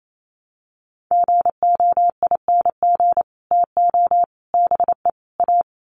Morse code, heard as a keyed tone, about 28 wpm, 700 Hz. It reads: GOING TO BE A
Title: Sets of 4 Words 28wpm